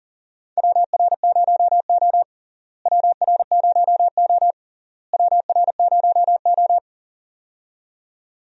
Morse code, heard as a keyed tone, about 40 wpm, 700 Hz. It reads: WR0O WR0O WR0O
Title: USA Call Signs - 4 Characters - Repeated 3x - 40wpm